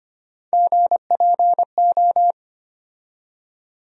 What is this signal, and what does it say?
Morse code, heard as a keyed tone, about 25 wpm, 700 Hz. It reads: GPO
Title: USA Call Sign Postfixes - 3 Letters 25wpm